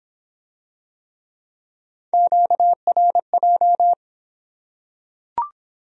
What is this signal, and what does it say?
Morse code, heard as a keyed tone, about 25 wpm, 700 Hz. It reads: QRJ